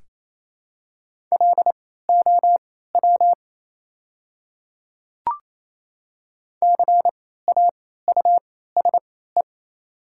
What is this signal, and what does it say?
Morse code, heard as a keyed tone, about 27 wpm, spaced as for 16 wpm, 700 Hz. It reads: LOW CAUSE
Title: 3x Character Spacing - Top 200 Words - 28wpm